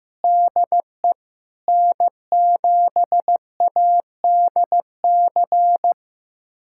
Morse code, heard as a keyed tone, about 15 wpm, 700 Hz. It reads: DE N7ADC
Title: USA Callsign DE Callsign - 5 Characters - 15wpm